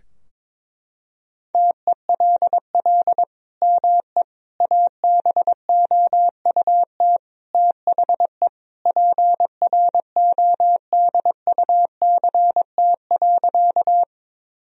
Morse code, heard as a keyed tone, about 22 wpm, 700 Hz. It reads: TELL ME ABOUT THE PRODUCT.
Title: Sentences from Top 1000 Words - Review Part 2 - 22wpm